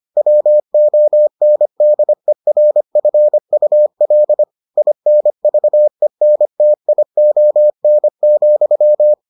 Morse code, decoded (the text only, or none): WONDERFUL INVENTION,